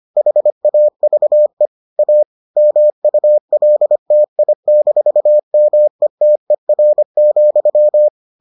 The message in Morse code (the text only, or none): HAVE A MULTI-METER,